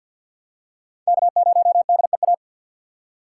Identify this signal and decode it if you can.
Morse code, heard as a keyed tone, about 52 wpm, 700 Hz. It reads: K0DEA
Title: USA Call Signs - 6x Character Spacing - 50wpm